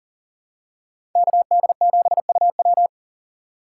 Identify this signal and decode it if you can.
Morse code, heard as a keyed tone, about 40 wpm, 700 Hz. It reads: KD7UW
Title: USA Call Signs - 8x Character Spacing - 40wpm